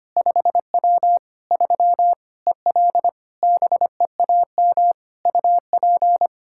Morse code, heard as a keyed tone, about 25 wpm, 700 Hz. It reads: 5W 3 EL BEAM UP
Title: Practice QSO - Exchange 2 - Info - 25wpm